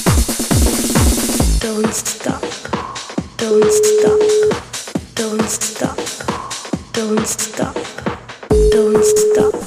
Ringback tone (Europe, 425 Hz), with noise.